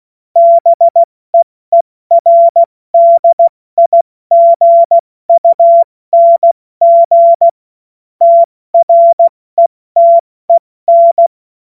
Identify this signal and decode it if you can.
Morse code, heard as a keyed tone, about 16 wpm, 680 Hz. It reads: BEERDIGUNG TRETEN
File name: Morsekurs / Witze / Witz 06. September 2022 70 F80